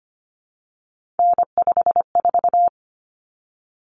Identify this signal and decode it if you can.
Morse code, heard as a keyed tone, about 25 wpm, 700 Hz. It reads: N54
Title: Common Cut Numbers - 3-Digit 25wpm